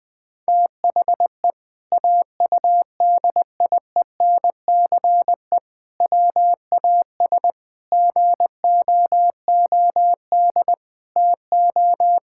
Morse code, heard as a keyed tone, about 20 wpm, 700 Hz. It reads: THE AUDIENCE WAS GOOD TO
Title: Sentences from Top 1000 Words - Review Part 5 - 20wpm